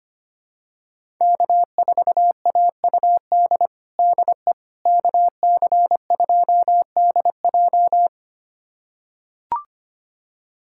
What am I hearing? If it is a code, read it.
Morse code, heard as a keyed tone, about 25 wpm, 700 Hz. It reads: K4AUD DE KC2DJ
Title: USA Callsign DE Callsign - 5 Characters - 25wpm